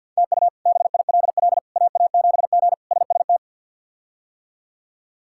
Morse code, heard as a keyed tone, about 50 wpm, 700 Hz. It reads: TU BILL AA7G SST
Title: K1USN SST QSO Practice - 50wpm